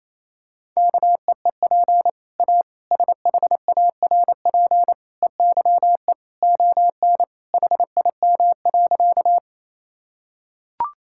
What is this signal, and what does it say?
Morse code, heard as a keyed tone, about 28 wpm, 700 Hz. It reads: KEEP A SHARP EYE ON HIM.
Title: Sentences from Top 1600 Words 28wpm